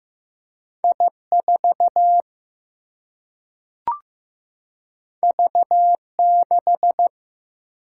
Morse code, heard as a keyed tone, about 15 wpm, 700 Hz. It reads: I 4 V 6
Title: Single-Letter Single-Number 15wpm